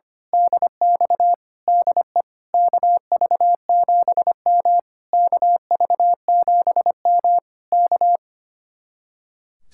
Morse code, heard as a keyed tone, about 24 wpm, 700 Hz. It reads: DX DE KV7M KV7M K
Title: Speed-Racing Common QSO Phrases - 25wpm